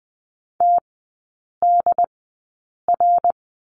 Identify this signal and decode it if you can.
Morse code, heard as a keyed tone, about 20 wpm, spaced as for 8 wpm, 700 Hz. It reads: TDR
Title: 100 Lesson 17e - Letter u - Three Characters 20wpm